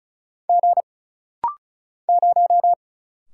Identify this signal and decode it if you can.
Morse code, heard as a keyed tone, about 33 wpm, 700 Hz. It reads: G 0
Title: Single Letter-Number Alt - Rapid-Fire 35wpm